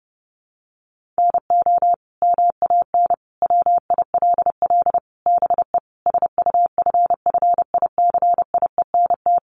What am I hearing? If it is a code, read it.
Morse code, heard as a keyed tone, about 30 wpm, 700 Hz. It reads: NO MAN WILL BE SUFFICIENT
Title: Sentences from Top 1400 Words 30wpm